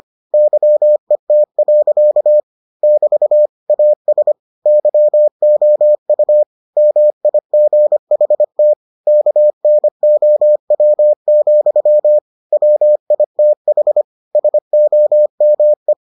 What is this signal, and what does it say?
Morse code, heard as a keyed tone, about 25 wpm, 600 Hz. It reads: YET. <BT> AS YOU MIGHT KNOW, WITH SOME